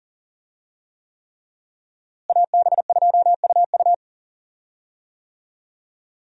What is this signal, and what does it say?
Morse code, heard as a keyed tone, about 40 wpm, 700 Hz. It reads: AB2UU
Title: USA Call Signs - 5 Characters - Repeated 3x - 40wpm